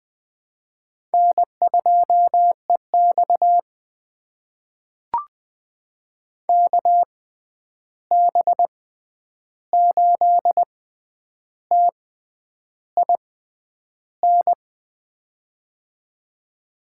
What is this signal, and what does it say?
Morse code, heard as a keyed tone, about 20 wpm, spaced as for 9 wpm, 700 Hz. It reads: N2EX KB8TIN
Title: USA Call Signs - 6x Character Spacing - 20wpm